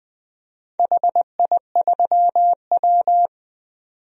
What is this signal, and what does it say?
Morse code, heard as a keyed tone, about 20 wpm, 700 Hz. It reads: HI3W
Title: DXCC-100 Call Signs - 4 Characters - 20wpm